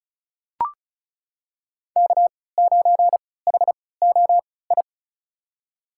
Morse code, heard as a keyed tone, about 34 wpm, spaced as for 20 wpm, 700 Hz. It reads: K9HOI
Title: USA Call Signs - 5 Characters - 3x Character Spacing - 35wpm